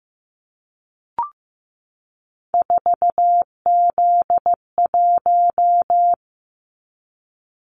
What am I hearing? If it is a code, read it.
Morse code, heard as a keyed tone, about 15 wpm, 700 Hz. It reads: 4Z1
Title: Callsign Prefixes for Top 50 DXCC - 15wpm